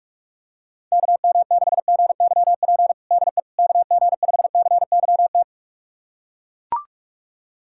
Morse code, heard as a keyed tone, about 42 wpm, 700 Hz. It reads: KM6GYP DE KG5CYT
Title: USA Callsign DE Callsign - 6 Characters - 45wpm